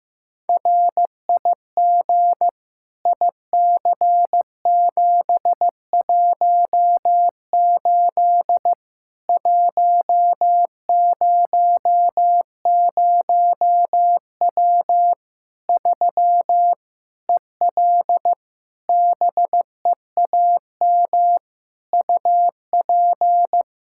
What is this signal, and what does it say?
Morse code, heard as a keyed tone, about 16 wpm, 700 Hz. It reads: RIG IC718 100W 3 EL BEAM UP
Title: Practice QSO - Exchange 2 - Info - 15wpm